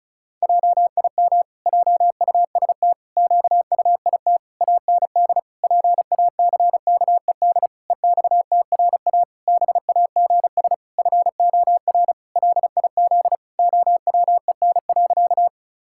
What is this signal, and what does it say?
Morse code, heard as a keyed tone, about 34 wpm, 700 Hz. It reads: JIM JUST QUIT AND PACKED EXTRA BAGS FOR LIZ OWEN.
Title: Pangram Challenge v2 - 35wpm